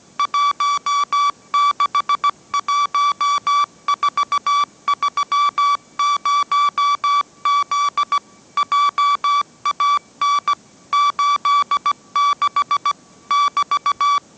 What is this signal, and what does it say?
Morse code, heard as a keyed tone, about 17 wpm, 1150 Hz. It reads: 161430Z JAN 86 <BT>
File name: mrx / mrx080 18wpm